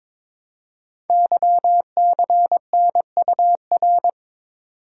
Morse code, heard as a keyed tone, about 22 wpm, 700 Hz. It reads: YCNUR
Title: Code Groups - 5 Letter:Numbers - 22wpm